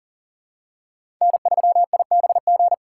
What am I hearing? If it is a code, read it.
Morse code, heard as a keyed tone, about 40 wpm, 700 Hz. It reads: N3IBG
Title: USA Call Signs - 5 Characters - 3x Character Spacing - 40wpm